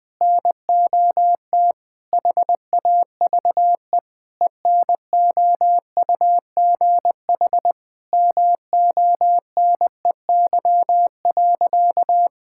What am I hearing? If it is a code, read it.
Morse code, heard as a keyed tone, about 21 wpm, 700 Hz. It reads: NOT HAVE ENOUGH MONEY.
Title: Sentences with 4-Word Phrases - Spoken Beginning - 20wpm